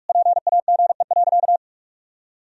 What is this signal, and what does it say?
Morse code, heard as a keyed tone, about 46 wpm, 700 Hz. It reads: PAGE.
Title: Sentences from Top 1000 Words - Review Part 1 - 45wpm